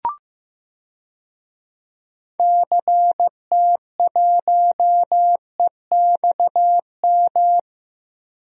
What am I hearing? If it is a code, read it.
Morse code, heard as a keyed tone, about 15 wpm, 700 Hz. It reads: CT1EXM
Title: DXCC-100 Call Signs - 6 Characters - Repeated 2x - 15wpm